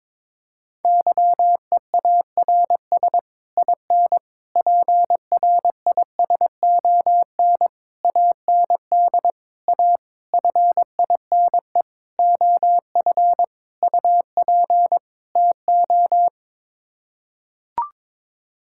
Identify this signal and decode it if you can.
Morse code, heard as a keyed tone, about 22 wpm, 700 Hz. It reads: YEARS IN PRISON AND A FINE OF UP TO
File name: Sets of 9 Words - 22wpm